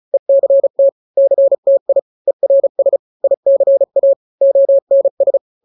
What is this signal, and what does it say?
Morse code, heard as a keyed tone, about 34 wpm, 550 Hz. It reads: ECT CTI ERS ICA ONS
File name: 3-grams 35 30 WPM